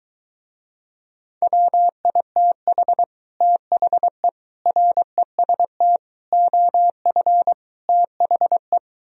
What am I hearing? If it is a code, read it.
Morse code, heard as a keyed tone, about 23 wpm, 700 Hz. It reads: WITH THE REST OF THE